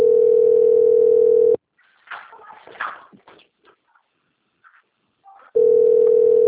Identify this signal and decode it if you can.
Ringback tone (North America), 440 + 480 Hz.
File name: out- - - - - .20525